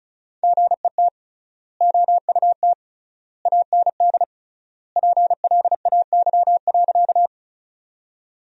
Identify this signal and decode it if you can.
Morse code, heard as a keyed tone, about 34 wpm, 700 Hz. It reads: GET OUT AND PLAY.
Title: 3x Word-Spacing - Sentences from Top 200 Words - 35wpm